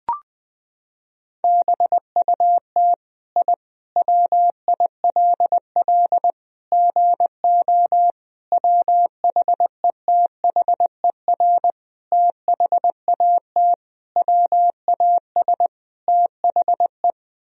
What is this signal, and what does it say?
Morse code, heard as a keyed tone, about 20 wpm, 700 Hz. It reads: BUT I WILL GO WHETHER THAT WAS THE
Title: Sentences from Top 1000 Words - Review Part 3 - 20wpm